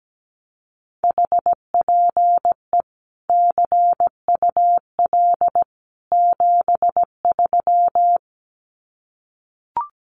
Morse code, heard as a keyed tone, about 17 wpm, 700 Hz. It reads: HPE CUL 73
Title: Common QSO Phrases 17wpm